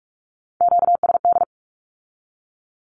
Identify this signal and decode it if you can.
Morse code, heard as a keyed tone, about 46 wpm, 700 Hz. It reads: QSD
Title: Q-Codes with Spoken Meaning - 45wpm